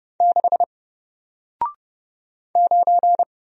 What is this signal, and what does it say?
Morse code, heard as a keyed tone, about 31 wpm, 700 Hz. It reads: 6 9
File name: Single-Letter-Number-30wpm